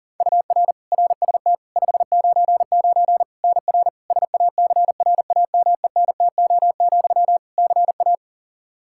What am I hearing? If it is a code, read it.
Morse code, heard as a keyed tone, about 40 wpm, 700 Hz. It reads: UR RST 599 NR SACRAMENTO, CA
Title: Practice QSO - Exchange 1 - Beginning - 40wpm